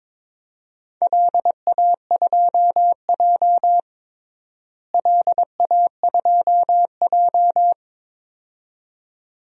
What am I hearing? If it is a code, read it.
Morse code, heard as a keyed tone, about 22 wpm, 700 Hz. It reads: LA2J LA2J
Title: DXCC-100 Call Signs - 4 Characters - Repeated 2x - 22wpm